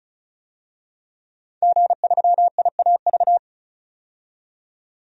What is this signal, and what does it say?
Morse code, heard as a keyed tone, about 34 wpm, 700 Hz. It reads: G3IAV
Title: DXCC-100 Call Signs - 5 Characters - 35wpm